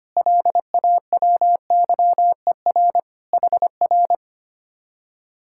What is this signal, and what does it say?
Morse code, heard as a keyed tone, about 24 wpm, 700 Hz. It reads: LAWYER HR
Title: Speed-Racing Common QSO Phrases Encore - 25wpm